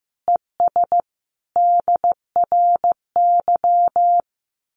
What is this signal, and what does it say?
Morse code, heard as a keyed tone, about 15 wpm, 700 Hz. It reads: ES DRY
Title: Common QSO Phrases 15wpm